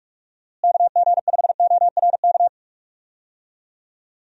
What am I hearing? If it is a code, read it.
Morse code, heard as a keyed tone, about 43 wpm, 700 Hz. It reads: KG5ORK
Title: USA Call Signs - 6 Characters - Repeated 2x - 45wpm